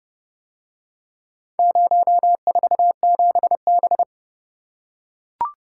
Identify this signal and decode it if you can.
Morse code, heard as a keyed tone, about 31 wpm, 700 Hz. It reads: 047B